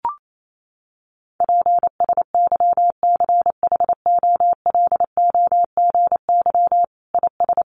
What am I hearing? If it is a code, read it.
Morse code, heard as a keyed tone, about 28 wpm, 700 Hz. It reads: PSYCHOLOGY IS
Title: Sentences from Top 1400 Words 28wpm